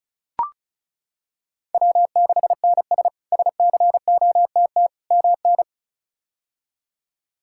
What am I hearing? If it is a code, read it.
Morse code, heard as a keyed tone, about 34 wpm, 700 Hz. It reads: W6NS SCOTT MN
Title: K1USN SST QSO Practice - 35wpm